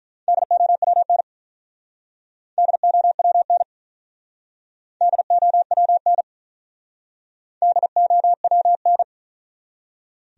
Morse code, heard as a keyed tone, about 45 wpm, 700 Hz. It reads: DOWN DOWN DOWN DOWN